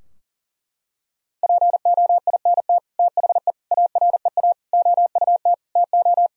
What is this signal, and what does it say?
Morse code, heard as a keyed tone, about 40 wpm, 700 Hz. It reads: POINT THE AREA OUT TO
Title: Sentences from Top 1000 Words - Review Part 2 - 40wpm